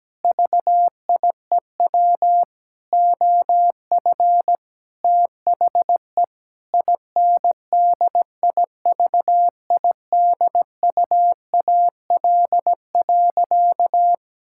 Morse code, heard as a keyed tone, about 17 wpm, 700 Hz. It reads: VIEW OF THE INDIVIDUAL.
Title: Sentences from Top 1000 Words - Review Part 6 - 17wpm